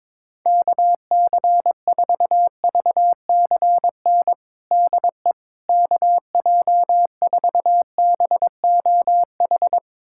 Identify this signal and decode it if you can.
Morse code, heard as a keyed tone, about 22 wpm, 700 Hz. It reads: KC4VCN DE KJ4BOH
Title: USA Callsign DE Callsign - 6 Characters - 22wpm